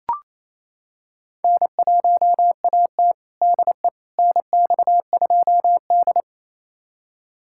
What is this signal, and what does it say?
Morse code, heard as a keyed tone, about 27 wpm, 700 Hz. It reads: N1AT DE NX2D
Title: USA Callsign DE Callsign - 4 Characters - 28wpm